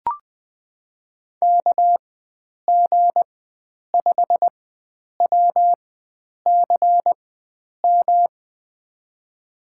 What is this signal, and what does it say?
Morse code, heard as a keyed tone, about 20 wpm, spaced as for 9 wpm, 700 Hz. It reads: KG5WCM
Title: USA Call Signs - 6 Characters - 4x Character Spacing - 20wpm